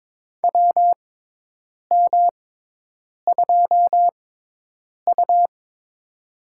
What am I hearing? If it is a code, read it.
Morse code, heard as a keyed tone, about 22 wpm, spaced as for 8 wpm, 700 Hz. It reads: WM2U
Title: USA Call Signs - 4 Characters - 6x Character Spacing - 22wpm